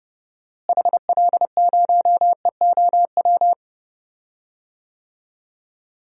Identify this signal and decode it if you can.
Morse code, heard as a keyed tone, about 32 wpm, 700 Hz. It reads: HL0EOW
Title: DXCC-100 Call Signs - 6 Characters - Repeated 2x - 30wpm